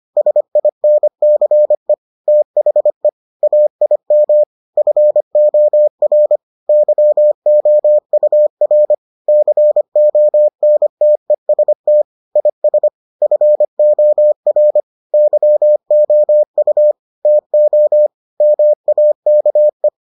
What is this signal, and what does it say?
Morse code, heard as a keyed tone, about 25 wpm, 600 Hz. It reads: SINCE THE AIM FOR YOUR CONTEST IS FOR YOU TO MAKE